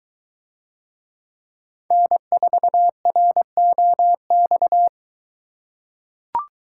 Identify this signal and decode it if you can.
Morse code, heard as a keyed tone, about 22 wpm, 700 Hz. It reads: N4ROX